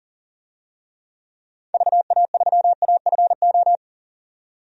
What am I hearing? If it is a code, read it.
Morse code, heard as a keyed tone, about 40 wpm, 700 Hz. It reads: VA3AFO
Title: DXCC-100 Call Signs - 6 Characters - 40wpm